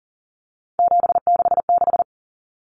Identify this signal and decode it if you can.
Morse code, heard as a keyed tone, about 38 wpm, 700 Hz. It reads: 766
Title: Common Cut Numbers - 3-Digit 40wpm